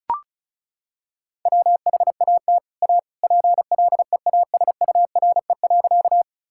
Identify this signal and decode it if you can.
Morse code, heard as a keyed tone, about 34 wpm, 700 Hz. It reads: WHAT A PLEASURE.
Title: Sentences from Top 1600 Words 35wpm